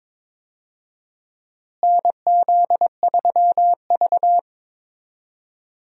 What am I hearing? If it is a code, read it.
Morse code, heard as a keyed tone, about 22 wpm, 700 Hz. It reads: NZ3V
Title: USA Call Signs - 4 Characters - Repeated 3x - 22wpm